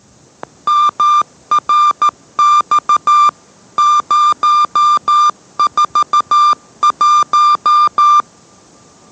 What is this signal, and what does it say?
Morse code, heard as a keyed tone, about 14 wpm, 1190 Hz. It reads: MRX 041